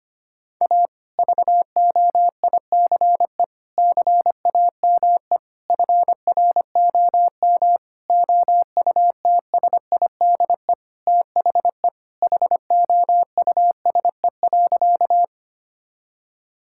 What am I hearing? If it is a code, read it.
Morse code, heard as a keyed tone, about 25 wpm, 700 Hz. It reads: A VOICE CAME FROM OUTSIDE THE HOUSE.
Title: Sentences from Top 1000 Words - Review Part 6 - 25wpm